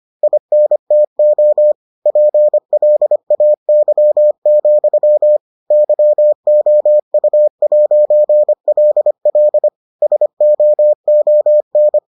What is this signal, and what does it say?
Morse code, heard as a keyed tone, about 25 wpm, 600 Hz. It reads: INTO PLAY, YOU'LL SOON